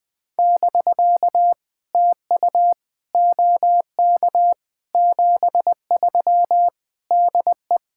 Morse code, heard as a keyed tone, about 20 wpm, 700 Hz. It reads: <BK> TU OK 73 DE
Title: POTA QSO Practice - 20wpm